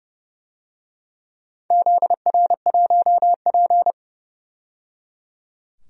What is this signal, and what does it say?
Morse code, heard as a keyed tone, about 30 wpm, 700 Hz. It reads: ZR1P
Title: DXCC-100 Call Signs - 4 Characters - 30wpm